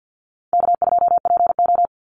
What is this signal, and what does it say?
Morse code, heard as a keyed tone, about 48 wpm, 700 Hz. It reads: K2PO
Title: Call Signs Repeated 3x - 50wpm